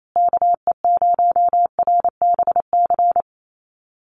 Morse code, heard as a keyed tone, about 28 wpm, 700 Hz. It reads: KE0RBC
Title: USA Call Signs Repeated 2x 28wpm